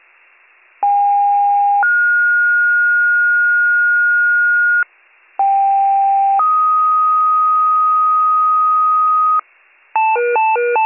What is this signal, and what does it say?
Two-tone page, heard as a one-second tone, then a three-second tone, about 810 then 1430 Hz, about 790 then 1250 Hz, with noise. It is heard from 0.8 to 9.4 s.